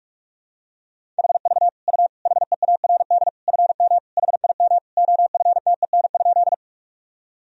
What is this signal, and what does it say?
Morse code, heard as a keyed tone, about 47 wpm, 700 Hz. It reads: HV U HEARD FM HIM OFTEN?
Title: Sentences from Top 400 Words and Abbreviations 45wpm